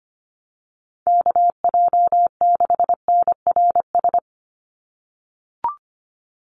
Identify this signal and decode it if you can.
Morse code, heard as a keyed tone, about 25 wpm, 700 Hz. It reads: KJ6NRS
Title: USA Call Signs 25wpm